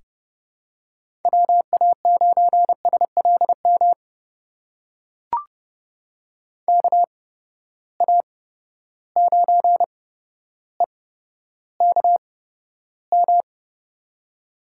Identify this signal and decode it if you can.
Morse code, heard as a keyed tone, about 30 wpm, spaced as for 13 wpm, 700 Hz. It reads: WA9SLM KA9EKM
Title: USA Call Signs - 6 Characters - 8x Character Spacing - 30wpm